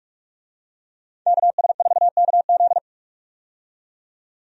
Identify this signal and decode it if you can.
Morse code, heard as a keyed tone, about 46 wpm, 700 Hz. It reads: KS4KZ
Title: USA Call Signs - 5 Characters - 45wpm